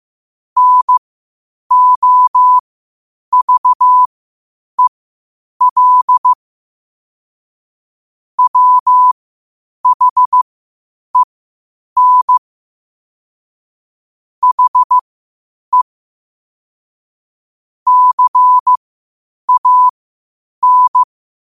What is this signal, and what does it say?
Morse code, heard as a keyed tone, about 15 wpm, spaced as for 8 wpm, 1000 Hz. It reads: NOVEL WHEN HE CAN
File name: qotdfiles / 8wpm